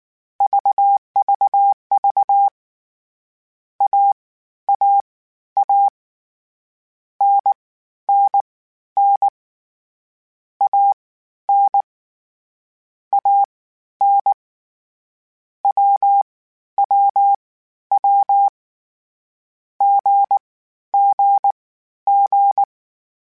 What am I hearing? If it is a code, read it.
Morse code, heard as a keyed tone, about 19 wpm, spaced as for 12 wpm, 800 Hz. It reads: VVV AAA NNN AN AN WWW GGG
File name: Short Runs / Short Runs-mp3s / Unballanced @18